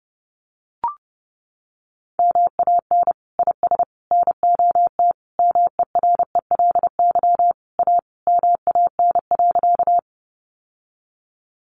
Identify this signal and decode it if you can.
Morse code, heard as a keyed tone, about 30 wpm, 700 Hz. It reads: MAN IS NOT MERELY A MAN.
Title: Sentences from Top 1400 Words 30wpm